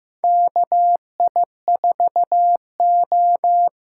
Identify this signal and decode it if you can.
Morse code, heard as a keyed tone, about 15 wpm, 700 Hz. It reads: KI4O
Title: USA Call Signs - 8x Character Spacing - 15wpm